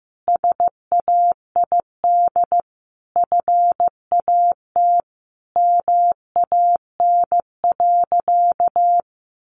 Morse code, heard as a keyed tone, about 15 wpm, 700 Hz. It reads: SAID FAT MAN.
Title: Sentences from Top 1100 Words 15wpm